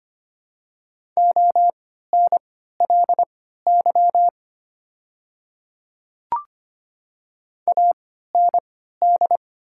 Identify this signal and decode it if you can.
Morse code, heard as a keyed tone, about 25 wpm, spaced as for 14 wpm, 700 Hz. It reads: ONLY AND
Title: 3x Character Spacing - Top 200 Words - 25wpm